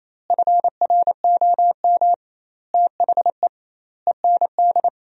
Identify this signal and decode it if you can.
Morse code, heard as a keyed tone, about 28 wpm, 700 Hz. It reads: FROM THE END
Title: 2x Word-Spacing - Sentences from Top 200 Words - 28wpm